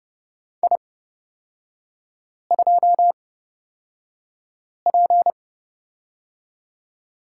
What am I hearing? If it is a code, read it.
Morse code, heard as a keyed tone, about 31 wpm, spaced as for 5 wpm, 700 Hz. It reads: I2P
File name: Lesson 36e - BT - Three Characters 30wpm